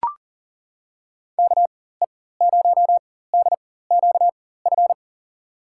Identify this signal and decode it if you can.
Morse code, heard as a keyed tone, about 40 wpm, spaced as for 19 wpm, 700 Hz. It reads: KE0DQF
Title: USA Call Signs - 6 Characters - 4x Character Spacing - 40wpm